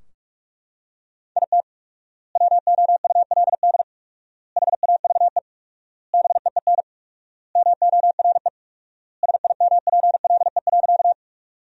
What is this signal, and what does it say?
Morse code, heard as a keyed tone, about 41 wpm, 700 Hz. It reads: IT WOULD HAVE BEEN MORE SIMPLE.
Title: 4x Word-Spacing - Sentences from Top 400 Words - 45wpm